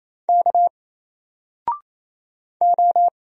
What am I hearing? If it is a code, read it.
Morse code, heard as a keyed tone, about 27 wpm, 700 Hz. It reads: K O
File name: Single-Letter-Number-28wpm